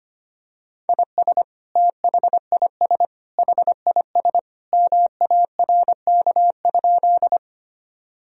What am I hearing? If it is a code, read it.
Morse code, heard as a keyed tone, about 25 wpm, 700 Hz. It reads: IS THIS HIS MARK?
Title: Sentences from Top 1000 Words - Review Part 1 - 25wpm